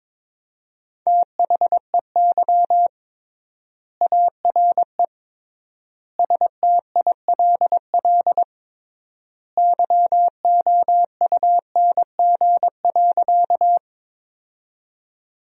Morse code, heard as a keyed tone, about 21 wpm, 700 Hz. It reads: THEY ARE STILL YOUNG.
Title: 3x Word-Spacing - Sentences from Top 400 Words - 22wpm